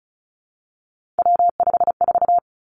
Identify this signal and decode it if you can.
Morse code, heard as a keyed tone, about 36 wpm, 700 Hz. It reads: W54
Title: Three Challenging Letter-Number Combinations 35wpm